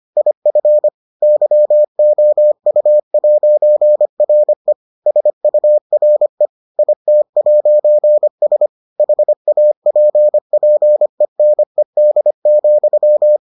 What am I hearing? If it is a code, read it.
Morse code, heard as a keyed tone, about 25 wpm, 600 Hz. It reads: IF YOU'RE SURE IT'S HAPPENED,